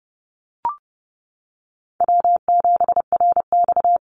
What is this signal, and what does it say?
Morse code, heard as a keyed tone, about 30 wpm, 700 Hz. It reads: W7RX
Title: Call Signs - 30wpm